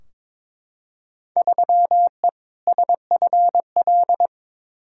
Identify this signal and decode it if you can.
Morse code, heard as a keyed tone, about 21 wpm, 700 Hz. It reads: 3E SFL
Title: Field Day Exchanges Encore - 22wpm